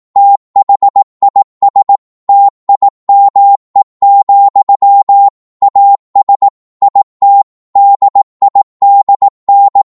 Morse code, heard as a keyed tone, about 18 wpm, 800 Hz. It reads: THIS TIME, AS IT DIDN